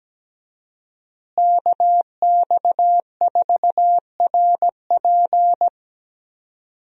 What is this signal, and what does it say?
Morse code, heard as a keyed tone, about 17 wpm, 700 Hz. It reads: KX4RP
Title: USA Call Signs - 5 Characters - Repeated 2x - 17wpm